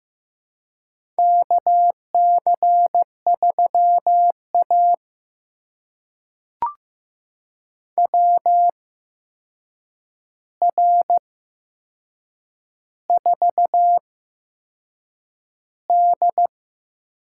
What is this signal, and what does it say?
Morse code, heard as a keyed tone, about 15 wpm, spaced as for 6 wpm, 700 Hz. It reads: KC3A WR4D
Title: USA Call Signs - 4 Characters - 8x Character Spacing - 15wpm